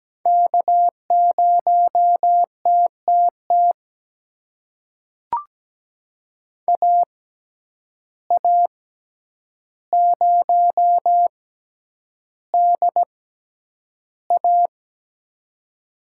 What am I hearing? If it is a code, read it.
Morse code, heard as a keyed tone, about 17 wpm, spaced as for 9 wpm, 700 Hz. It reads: K0TTT AA0DA
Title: USA Call Signs - 5 Characters - 6x Character Spacing - 17wpm